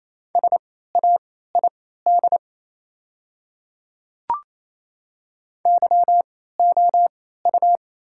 Morse code, heard as a keyed tone, about 28 wpm, spaced as for 16 wpm, 700 Hz. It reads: SAID YOU
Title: 3x Character Spacing - Top 300 Words - 28wpm